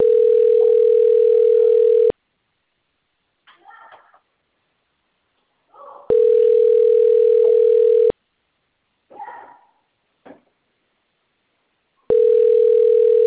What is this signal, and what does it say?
Ringback tone (North America), 440 + 480 Hz.